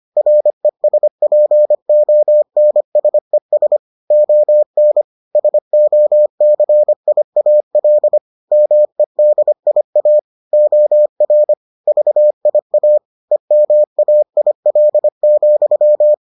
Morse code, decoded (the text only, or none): RESPONSES ON SOCIAL MEDIA OR VIA EMAIL,